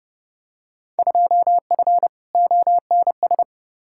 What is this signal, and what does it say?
Morse code, heard as a keyed tone, about 32 wpm, 700 Hz. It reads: 2F ONS
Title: Field Day Exchanges - 30wpm